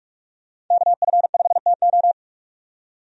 Morse code, heard as a keyed tone, about 46 wpm, 700 Hz. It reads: KF5TO
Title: USA Call Signs - 5 Characters - 45wpm